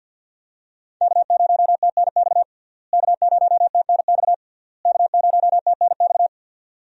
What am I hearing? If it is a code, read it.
Morse code, heard as a keyed tone, about 46 wpm, 700 Hz. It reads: K0TNX K0TNX K0TNX
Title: USA Call Signs - 5 Characters - Repeated 3x - 50wpm